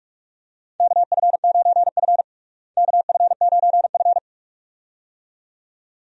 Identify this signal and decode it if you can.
Morse code, heard as a keyed tone, about 46 wpm, 700 Hz. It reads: KF9F KF9F
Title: USA Call Signs - 4 Characters - Repeated 2x - 45wpm